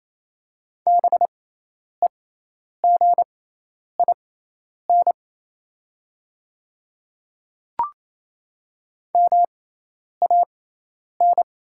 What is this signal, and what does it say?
Morse code, heard as a keyed tone, about 28 wpm, spaced as for 10 wpm, 700 Hz. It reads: BEGIN MAN
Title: 6x Character Spacing - Top 300 Words - 28wpm